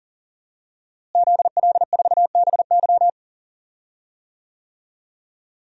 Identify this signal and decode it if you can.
Morse code, heard as a keyed tone, about 40 wpm, 700 Hz. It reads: ZL4BY
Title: DXCC-100 Call Signs - 5 Characters - Repeated 2x - 40wpm